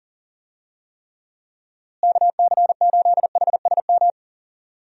Morse code, heard as a keyed tone, about 40 wpm, 700 Hz. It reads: KC8HSM
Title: USA Call Signs - 6 Characters - Repeated 2x - 40wpm